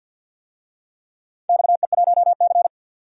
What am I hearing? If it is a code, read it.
Morse code, heard as a keyed tone, about 53 wpm, 700 Hz. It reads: XE1C
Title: DXCC-100 Call Signs - 4 Characters - 50wpm